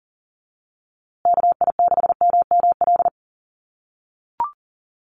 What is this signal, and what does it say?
Morse code, heard as a keyed tone, about 40 wpm, 700 Hz. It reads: KI6MML
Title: USA Call Signs Repeated 3x 40wpm